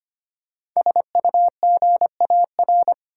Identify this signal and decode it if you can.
Morse code, heard as a keyed tone, about 25 wpm, 700 Hz. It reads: SUGAR
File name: 5 Letter Words - First 3 Letters Given - 25wpm